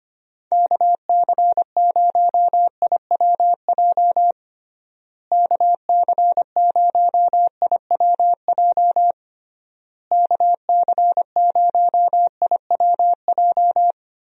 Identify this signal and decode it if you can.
Morse code, heard as a keyed tone, about 25 wpm, 700 Hz. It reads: KC0IWJ KC0IWJ KC0IWJ
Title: USA Call Signs - 6 Characters - Repeated 3x - 25wpm